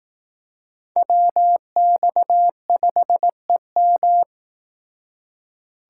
Morse code, heard as a keyed tone, about 18 wpm, 700 Hz. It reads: WX5EM
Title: Speed Racing - USA Call Signs 15wpm